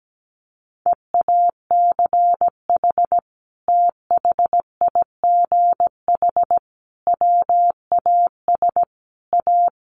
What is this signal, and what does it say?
Morse code, heard as a keyed tone, about 17 wpm, 700 Hz. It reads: EACH THIGH WAS A
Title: A Sound Of Thunder 17wpm